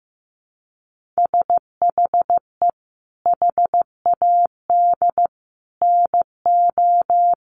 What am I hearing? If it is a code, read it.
Morse code, heard as a keyed tone, about 15 wpm, 700 Hz. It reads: SHE HAD NO
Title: Sentences from Top 1400 Words 15wpm